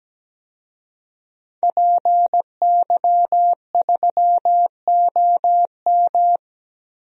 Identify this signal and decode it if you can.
Morse code, heard as a keyed tone, about 17 wpm, 700 Hz. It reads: PY3OM
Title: DXCC-100 Call Signs - 5 Characters - 17wpm